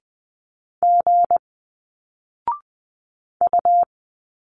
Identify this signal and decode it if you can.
Morse code, heard as a keyed tone, about 20 wpm, 700 Hz. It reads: G U
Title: Single Letters 20wpm